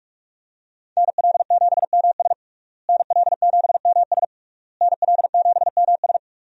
Morse code, heard as a keyed tone, about 46 wpm, 700 Hz. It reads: NL7MS NL7MS NL7MS
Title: USA Call Signs - 5 Characters - Repeated 3x - 45wpm